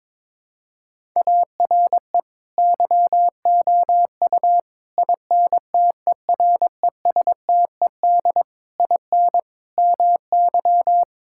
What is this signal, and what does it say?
Morse code, heard as a keyed tone, about 22 wpm, 700 Hz. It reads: ARE YOU INTERESTED IN MY
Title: Sentences from Top 1000 Words - Review Part 6 - 22wpm